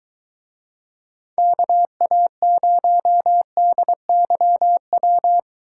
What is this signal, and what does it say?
Morse code, heard as a keyed tone, about 23 wpm, 700 Hz. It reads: KA0DYW